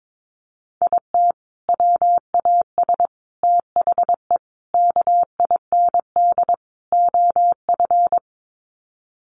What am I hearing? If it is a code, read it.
Morse code, heard as a keyed tone, about 22 wpm, 700 Hz. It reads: IT WAS THE KIND OF
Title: Sets of 5 Words Encore2 22wpm